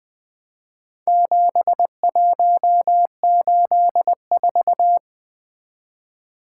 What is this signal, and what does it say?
Morse code, heard as a keyed tone, about 20 wpm, 700 Hz. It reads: 7184
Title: Four Letter-Number Combinations - 20wpm